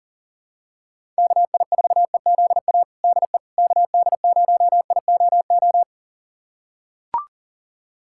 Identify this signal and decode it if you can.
Morse code, heard as a keyed tone, about 37 wpm, 700 Hz. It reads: KI4EZA DE KD0IOO
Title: USA Callsign DE Callsign - 6 Characters - 40wpm